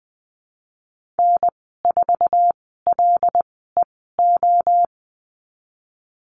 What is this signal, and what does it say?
Morse code, heard as a keyed tone, about 20 wpm, spaced as for 15 wpm, 700 Hz. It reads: N4LEO
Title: 122 Lesson 20g - Number 3 - Call Signs 20wpm